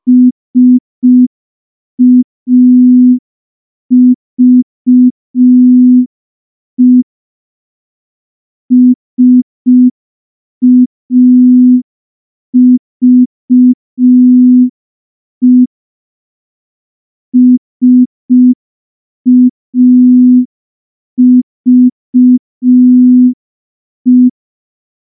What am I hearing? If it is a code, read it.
Morse code, heard as a keyed tone, about 5 wpm, 250 Hz. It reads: SAVE SAVE SAVE
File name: morse / morse-training-05WPM